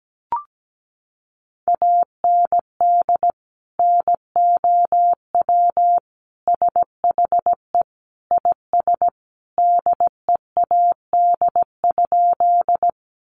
Morse code, heard as a keyed tone, about 17 wpm, 700 Hz. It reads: AND NOW SHE IS DEAD?
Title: Sentences from Top 1500 Words 17wpm